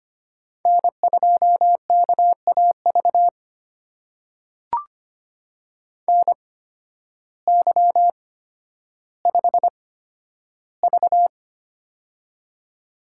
Morse code, heard as a keyed tone, about 25 wpm, spaced as for 12 wpm, 700 Hz. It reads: N2KAV NY5V
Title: USA Call Signs - 8x Character Spacing - 25wpm